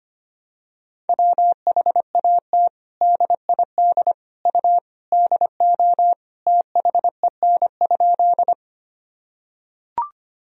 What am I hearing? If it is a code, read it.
Morse code, heard as a keyed tone, about 25 wpm, 700 Hz. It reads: WHAT DID U DO THEN?
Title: Sentences from Top 100 Words and Abbreviations 25wpm